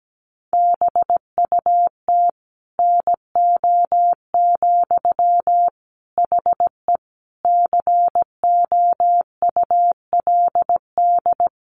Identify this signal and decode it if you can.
Morse code, heard as a keyed tone, about 17 wpm, 700 Hz. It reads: BUT NO, HE COULD
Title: Sentences from Top 1400 Words 17wpm